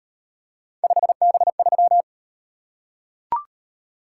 Morse code, heard as a keyed tone, about 38 wpm, 700 Hz. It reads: 5B3